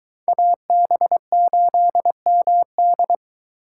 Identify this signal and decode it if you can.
Morse code, heard as a keyed tone, about 23 wpm, 700 Hz. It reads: AB8MD